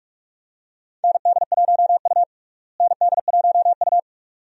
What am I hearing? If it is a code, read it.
Morse code, heard as a keyed tone, about 46 wpm, 700 Hz. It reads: ND1U ND1U
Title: USA Call Signs - 4 Characters - Repeated 2x - 45wpm